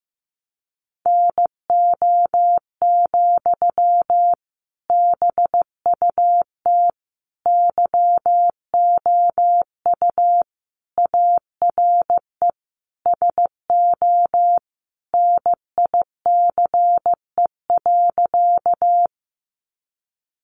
Morse code, heard as a keyed tone, about 15 wpm, 700 Hz. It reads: NO, BUT YOU ARE SO NICE.
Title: Sentences from Top 1100 Words 15wpm